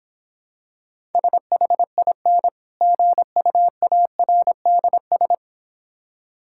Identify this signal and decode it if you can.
Morse code, heard as a keyed tone, about 26 wpm, 700 Hz. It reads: SHIN GUARDS